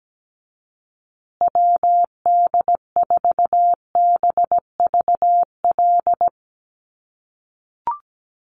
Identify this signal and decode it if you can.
Morse code, heard as a keyed tone, about 17 wpm, 700 Hz. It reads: WD4BVL
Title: USA Call Signs Repeated 2x 17wpm